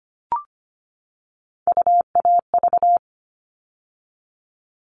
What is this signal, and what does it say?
Morse code, heard as a keyed tone, about 25 wpm, 700 Hz. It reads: UAV
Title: All Cut Numbers - 3-Digit 25wpm